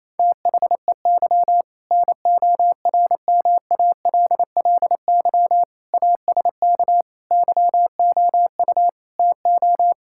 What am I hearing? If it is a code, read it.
Morse code, heard as a keyed tone, about 28 wpm, 700 Hz. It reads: THEY NORMALLY ASK YOU TO
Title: Sentences from Top 1000 Words - Review Part 6 - 28wpm